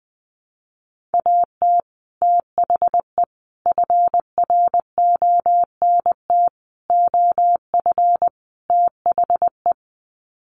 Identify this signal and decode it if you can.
Morse code, heard as a keyed tone, about 20 wpm, 700 Hz. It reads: AT THE FRONT OF THE
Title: Sets of 5 Words 20wpm